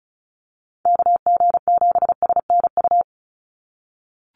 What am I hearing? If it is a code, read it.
Morse code, heard as a keyed tone, about 34 wpm, 700 Hz. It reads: KG7SNU
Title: USA Call Signs 35wpm